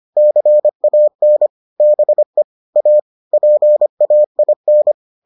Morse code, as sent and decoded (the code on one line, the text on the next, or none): -.-. .- -. / -... / .- / .--. .- .. -.
CAN BE A PAIN